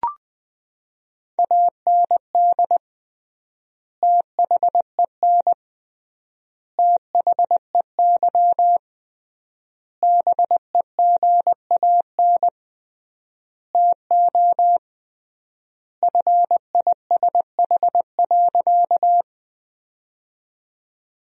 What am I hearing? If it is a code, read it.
Morse code, heard as a keyed tone, about 21 wpm, 700 Hz. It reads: AND THEN THEY BEGAN TO FISH.
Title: 3x Word-Spacing - Sentences from Top 300 Words - 20wpm